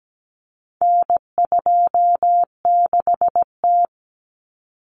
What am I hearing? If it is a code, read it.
Morse code, heard as a keyed tone, about 17 wpm, 700 Hz. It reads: N26T
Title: Common Cut Numbers - 4-Digit 17wpm